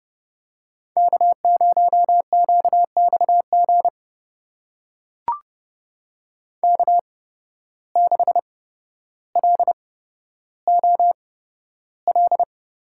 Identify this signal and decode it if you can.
Morse code, heard as a keyed tone, about 30 wpm, spaced as for 13 wpm, 700 Hz. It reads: K0QXG K6LOL
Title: USA Call Signs - 5 Characters - 8x Character Spacing - 30wpm